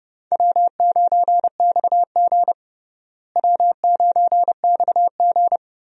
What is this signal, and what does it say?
Morse code, heard as a keyed tone, about 30 wpm, 700 Hz. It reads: W9XG W9XG
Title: USA Call Signs - 4 Characters - Repeated 2x - 30wpm